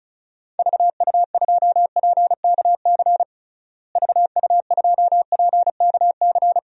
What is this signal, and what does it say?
Morse code, heard as a keyed tone, about 34 wpm, 700 Hz. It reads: VU2PKC VU2PKC
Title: DXCC-100 Call Signs - 6 Characters - Repeated 3x - 35wpm